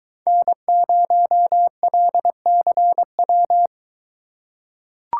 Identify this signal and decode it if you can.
Morse code, heard as a keyed tone, about 22 wpm, 700 Hz. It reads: N0LCW